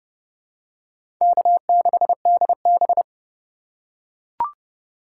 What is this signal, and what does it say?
Morse code, heard as a keyed tone, about 32 wpm, 700 Hz. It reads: K6DB
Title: USA Call Signs - 4 Characters - 4x Character Spacing - 30wpm